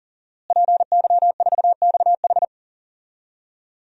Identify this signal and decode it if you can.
Morse code, heard as a keyed tone, about 40 wpm, 700 Hz. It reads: PY4XH
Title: DXCC-100 Call Signs - 5 Characters - Repeated 3x - 40wpm